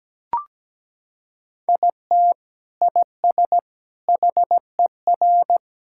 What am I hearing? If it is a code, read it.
Morse code, heard as a keyed tone, about 17 wpm, 700 Hz. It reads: IT IS HER
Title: Sentences from Top 1600 Words 17wpm